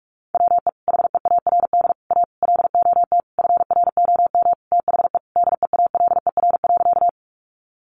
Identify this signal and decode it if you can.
Morse code, heard as a keyed tone, about 44 wpm, 700 Hz. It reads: WE HEARD A LOT FROM THE DEALER.
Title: Sentences from Top 1100 Words 45wpm